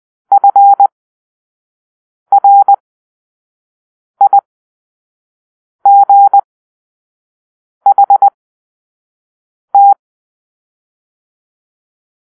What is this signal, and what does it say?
Morse code, heard as a keyed tone, about 20 wpm, spaced as for 5 wpm, 800 Hz. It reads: FRIGHT
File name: arrl / Grp-4 Drill-1